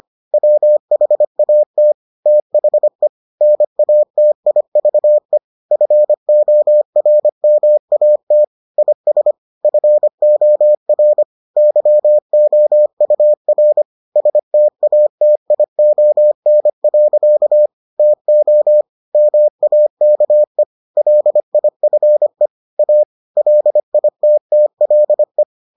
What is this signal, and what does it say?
Morse code, heard as a keyed tone, about 25 wpm, 600 Hz. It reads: WHAT THE NATIVE FORMAT IS FOR YOUR STATION. TO MAKE LIFE A LITTLE